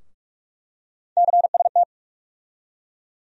Morse code, heard as a keyed tone, about 43 wpm, 700 Hz. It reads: CST
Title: USA Call Sign Postfixes - 3 Letters 45wpm